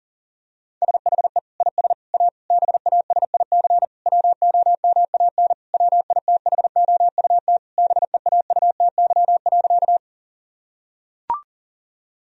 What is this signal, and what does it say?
Morse code, heard as a keyed tone, about 40 wpm, 700 Hz. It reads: SHE IS A BASIC WOMAN WITHOUT BEAUTY.
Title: Sentences from Top 1000 Words - Review Part 3 - 40wpm